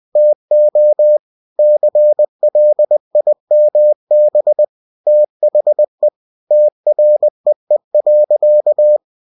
Morse code, heard as a keyed tone, about 20 wpm, 600 Hz. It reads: TO CLIMB THE TREE.
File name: cw-books / winnie-the-pooh / winnie-20wpm-page-0002